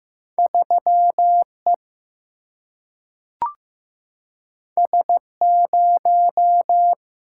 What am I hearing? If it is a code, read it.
Morse code, heard as a keyed tone, about 15 wpm, 700 Hz. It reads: 3 E S 0
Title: Single-Letter Single-Number 15wpm